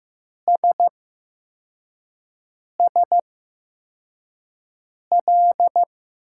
Morse code, heard as a keyed tone, about 15 wpm, spaced as for 4 wpm, 700 Hz. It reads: SSL
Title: USA Call Signs - 5 Characters - 8x Character Spacing - 15wpm